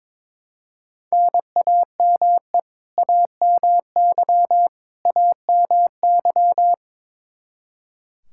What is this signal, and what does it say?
Morse code, heard as a keyed tone, about 21 wpm, 700 Hz. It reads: NAME AMY AMY
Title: Practice QSO - Exchange 1 - Beginning - 22wpm